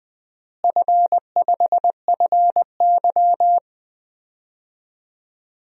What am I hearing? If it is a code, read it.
Morse code, heard as a keyed tone, about 21 wpm, 700 Hz. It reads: F5FY
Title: DXCC-100 Call Signs - 4 Characters - Repeated 2x - 20wpm